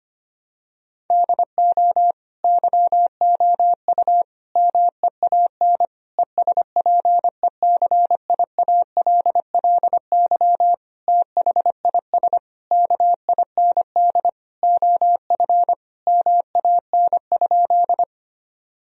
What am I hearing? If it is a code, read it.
Morse code, heard as a keyed tone, about 25 wpm, 700 Hz. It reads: DO YOU MEAN ESPECIALLY THIS KIND OF MAN?
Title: Sentences from Top 1000 Words - Review Part 4 - 25wpm